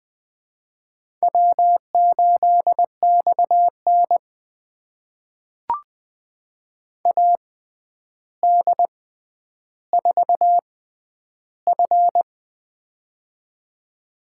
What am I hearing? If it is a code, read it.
Morse code, heard as a keyed tone, about 20 wpm, spaced as for 10 wpm, 700 Hz. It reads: W8XN AD4F
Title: USA Call Signs - 4 Characters - 6x Character Spacing - 20wpm